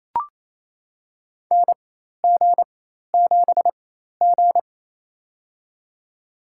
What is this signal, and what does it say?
Morse code, heard as a keyed tone, about 28 wpm, spaced as for 13 wpm, 700 Hz. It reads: NG7G
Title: USA Call Signs - 4 Characters - 4x Character Spacing - 28wpm